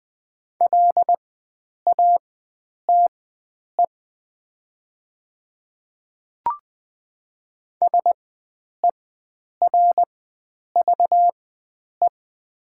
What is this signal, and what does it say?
Morse code, heard as a keyed tone, about 20 wpm, spaced as for 9 wpm, 700 Hz. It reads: LATE SERVE
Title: 4x Character Spacing - Top 500 Words - 20wpm